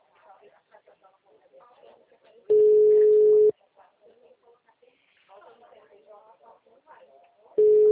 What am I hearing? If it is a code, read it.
Ringback tone (Europe, 425 Hz), with speech.